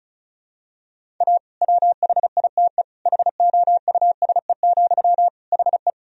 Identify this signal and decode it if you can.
Morse code, heard as a keyed tone, about 34 wpm, 700 Hz. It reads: A WHITE HOUSE, HE
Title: Sentences from Top 1000 Words - Review Part 1 - 35wpm